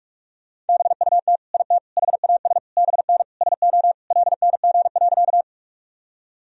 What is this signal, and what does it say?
Morse code, heard as a keyed tone, about 46 wpm, 700 Hz. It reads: BUT IT HAS BN SO LNG.
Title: Sentences from Top 100 Words and Abbreviations 45wpm